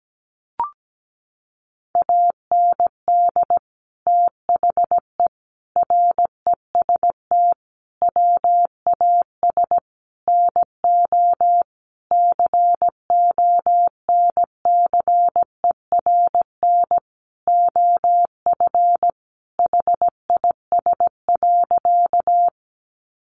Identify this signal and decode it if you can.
Morse code, heard as a keyed tone, about 17 wpm, 700 Hz. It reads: AND THE REST WAS NO CONCERN OF HIS.
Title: Sentences from Top 1000 Words 17wpm